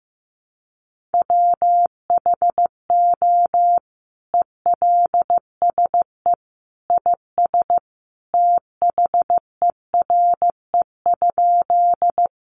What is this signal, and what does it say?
Morse code, heard as a keyed tone, about 15 wpm, 700 Hz. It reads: WHO ELSE IS THERE?
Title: Sentences from Top 800 Words 15wpm